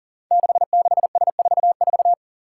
Morse code, heard as a keyed tone, about 42 wpm, 700 Hz. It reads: 66S44
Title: Five Challenging Letter-Number Combinations 40wpm